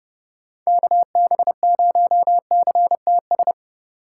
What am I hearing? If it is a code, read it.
Morse code, heard as a keyed tone, about 30 wpm, 700 Hz. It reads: KB0CTS
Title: USA Call Signs - 6 Characters - 4x Character Spacing - 30wpm